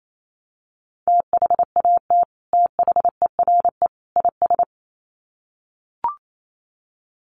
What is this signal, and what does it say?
Morse code, heard as a keyed tone, about 28 wpm, 700 Hz. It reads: THAT THERE IS
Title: Sets of 3 Words 28wpm